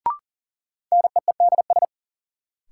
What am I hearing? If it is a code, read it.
Morse code, heard as a keyed tone, about 40 wpm, 700 Hz. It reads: NEEDS
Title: Sentences from Top 1000 Words - Review Part 3 - New Word Drill - 40wpm